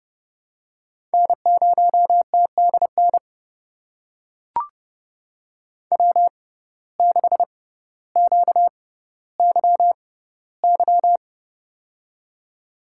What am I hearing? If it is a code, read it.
Morse code, heard as a keyed tone, about 30 wpm, spaced as for 15 wpm, 700 Hz. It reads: N0TDN W6QYY
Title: USA Call Signs - 5 Characters - 6x Character Spacing - 30wpm